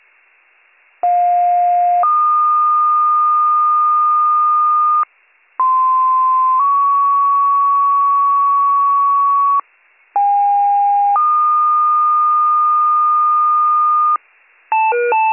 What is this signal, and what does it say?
Two-tone page, heard as a one-second tone, then a three-second tone, about 710 then 1220 Hz, about 1040 then 1120 Hz, about 810 then 1220 Hz. It is heard from 1.0 to 14.2 s.